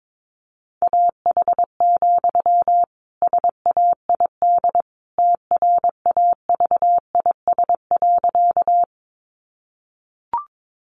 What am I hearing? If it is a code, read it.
Morse code, heard as a keyed tone, about 22 wpm, 700 Hz. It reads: AH, SAID TRAVIS.
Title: A Sound Of Thunder 22wpm